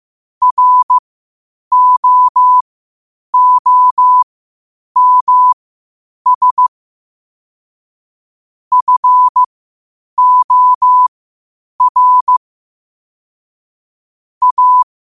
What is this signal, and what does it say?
Morse code, heard as a keyed tone, about 15 wpm, spaced as for 8 wpm, 1000 Hz. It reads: ROOMS FOR A
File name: qotdfiles / 8wpm